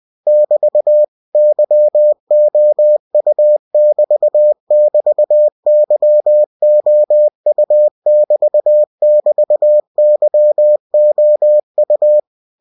Morse code, heard as a keyed tone, about 20 wpm, 600 Hz. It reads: <BT> YOU<BT><BT>YOU<BT><BT>YOU
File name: cw-books / winnie-the-pooh / winnie-20wpm-page-0072